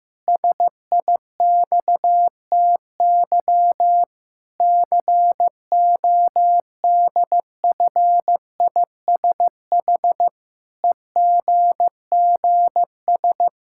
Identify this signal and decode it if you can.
Morse code, heard as a keyed tone, about 16 wpm, 700 Hz. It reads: SIXTY CODFISH EGGS
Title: Pangram Challenge v2 - 15wpm